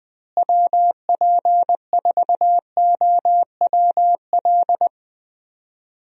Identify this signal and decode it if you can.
Morse code, heard as a keyed tone, about 20 wpm, 700 Hz. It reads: WP4OWL
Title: USA Call Signs - 6 Characters - Repeated 2x - 20wpm